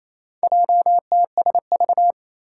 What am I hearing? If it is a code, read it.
Morse code, heard as a keyed tone, about 28 wpm, 700 Hz. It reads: JTSV
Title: Four Challenging Letter-Number Combinations 28wpm